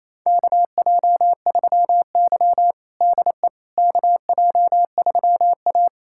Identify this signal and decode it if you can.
Morse code, heard as a keyed tone, about 28 wpm, 700 Hz. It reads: KJ3Y DE KJ3A
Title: USA Callsign DE Callsign - 4 Characters - 28wpm